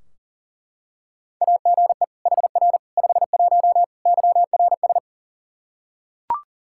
Morse code, heard as a keyed tone, about 37 wpm, 700 Hz. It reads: AGE HR 51 YRS
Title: Practice QSO - Exchange 3 - More Info - 40wpm